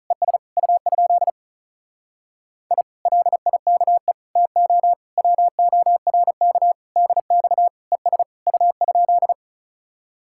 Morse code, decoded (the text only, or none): ES U? I LIKE TO WORK DX ES U?